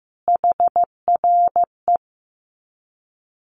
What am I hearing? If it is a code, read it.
Morse code, heard as a keyed tone, about 14 wpm, 700 Hz. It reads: HRE
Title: Common QSO Phrases 15wpm